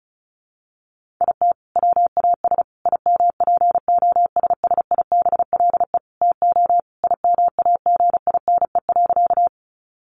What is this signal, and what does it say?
Morse code, heard as a keyed tone, about 34 wpm, 700 Hz. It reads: IT WAS IMPOSSIBLE TO IMAGINE.
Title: Sentences from Top 1400 Words 35wpm